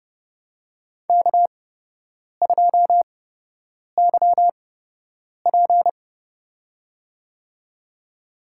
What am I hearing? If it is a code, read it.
Morse code, heard as a keyed tone, about 30 wpm, spaced as for 8 wpm, 700 Hz. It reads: K2YP
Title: USA Call Signs - 8x Character Spacing - 30wpm